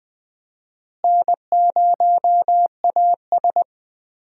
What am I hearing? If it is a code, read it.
Morse code, heard as a keyed tone, about 20 wpm, 700 Hz. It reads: N0AS
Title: USA Call Signs - 4 Characters - Repeated 2x - 20wpm